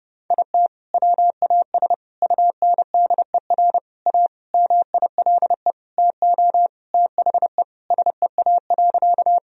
Morse code, heard as a keyed tone, about 30 wpm, 700 Hz. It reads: IT WAS UNDER A MILE TO THE SEA.
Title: Sentences from Top 1000 Words - Review Part 1 - 30wpm